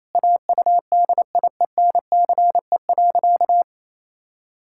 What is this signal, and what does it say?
Morse code, heard as a keyed tone, about 28 wpm, 700 Hz. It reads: AUDIENCE.
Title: Pangram Challenge v2 - 28wpm